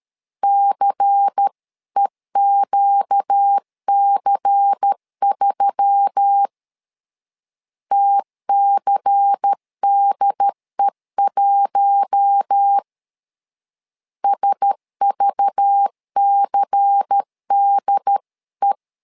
Morse code, heard as a keyed tone, about 13 wpm, 800 Hz. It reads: CEQC3 TCDE1 SVCDE
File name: LEZIONI / 5 LBVC / 5 LBVC 12wpm 1955